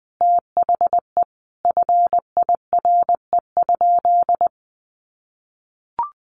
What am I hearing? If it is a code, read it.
Morse code, heard as a keyed tone, about 20 wpm, 700 Hz. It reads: THE FIRE?
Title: Sentences from Top 1000 Words 20wpm